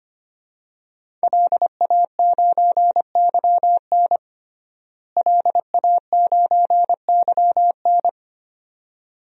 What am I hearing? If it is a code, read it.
Morse code, heard as a keyed tone, about 25 wpm, 700 Hz. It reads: LA9YN LA9YN
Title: DXCC-100 Call Signs - 5 Characters - Repeated 2x - 25wpm